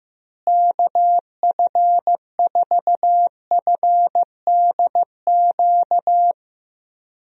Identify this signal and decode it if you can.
Morse code, heard as a keyed tone, about 15 wpm, 700 Hz. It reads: KF4FDQ
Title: USA Call Signs - 6 Characters - Repeated 3x - 15wpm